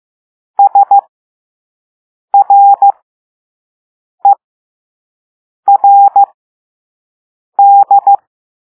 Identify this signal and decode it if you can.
Morse code, heard as a keyed tone, about 15 wpm, spaced as for 5 wpm, 800 Hz. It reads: SRERD
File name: Morse / Level3-All-B